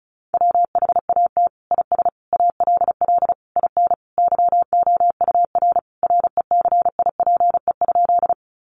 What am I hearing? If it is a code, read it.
Morse code, heard as a keyed tone, about 34 wpm, 700 Hz. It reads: WHAT IS ALL IN YOUR RECIPE?
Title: Sentences from Top 1200 Words 35wpm